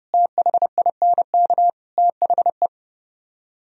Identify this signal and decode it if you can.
Morse code, heard as a keyed tone, about 37 wpm, 700 Hz. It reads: THINK THE
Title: Speed-Racing - Sets of 4 Words 30wpm